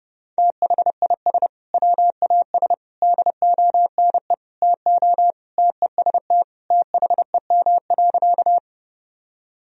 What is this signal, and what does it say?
Morse code, heard as a keyed tone, about 30 wpm, 700 Hz. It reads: THIS WAS DONE TO TEST THEM.
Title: Sentences from Top 1000 Words - Review Part 3 - 30wpm